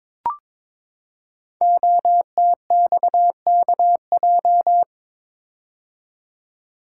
Morse code, heard as a keyed tone, about 22 wpm, 700 Hz. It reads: OTXKJ
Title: Code Groups - 5 Letter:Numbers - 22wpm